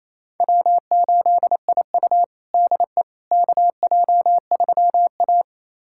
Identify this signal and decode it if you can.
Morse code, heard as a keyed tone, about 28 wpm, 700 Hz. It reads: W8IU DE KJ3A
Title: USA Callsign DE Callsign - 4 Characters - 28wpm